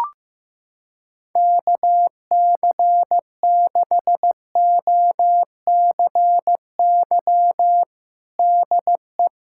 Morse code, heard as a keyed tone, about 15 wpm, 700 Hz. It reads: KC6OCY DE
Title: USA Callsign DE Callsign - 6 Characters - 15wpm